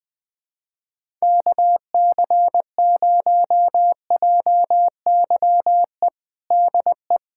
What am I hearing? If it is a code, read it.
Morse code, heard as a keyed tone, about 20 wpm, 700 Hz. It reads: KC0JYE DE
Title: USA Callsign DE Callsign - 6 Characters - 20wpm